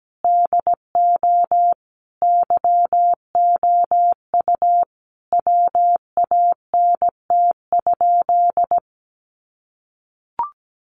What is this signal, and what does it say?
Morse code, heard as a keyed tone, about 17 wpm, 700 Hz. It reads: DO YOU WANT?
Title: Sentences from Top 200 Words 17wpm